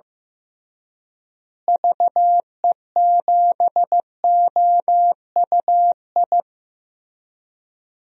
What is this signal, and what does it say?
Morse code, heard as a keyed tone, about 16 wpm, 700 Hz. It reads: VE7OUI
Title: DXCC-100 Call Signs - 6 Characters - Repeated 3x - 15wpm